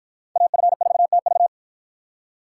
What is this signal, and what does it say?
Morse code, heard as a keyed tone, about 53 wpm, 700 Hz. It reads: AF4TV